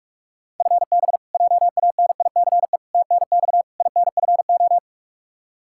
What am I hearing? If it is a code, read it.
Morse code, heard as a keyed tone, about 46 wpm, 700 Hz. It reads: FB JANICE TNX INFO
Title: K1USN SST QSO Practice 45wpm